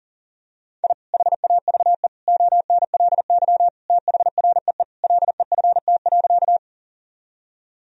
Morse code, heard as a keyed tone, about 40 wpm, 700 Hz. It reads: I HAVE ONLY THREE LEFT.
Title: Sentences from Top 1000 Words - Review Part 1 - 40wpm